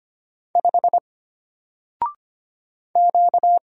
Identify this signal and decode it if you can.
Morse code, heard as a keyed tone, about 25 wpm, 700 Hz. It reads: 5 Q
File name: Single-Letter-Number-25wpm